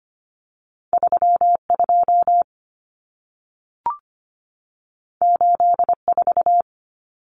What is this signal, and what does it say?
Morse code, heard as a keyed tone, about 25 wpm, 700 Hz. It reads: 32 84
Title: Numbers - 2-Digit 25wpm